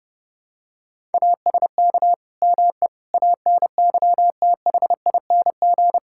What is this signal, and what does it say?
Morse code, heard as a keyed tone, about 32 wpm, 700 Hz. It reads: ASK ME ANYTHING
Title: Sentences from Top 1600 Words 30wpm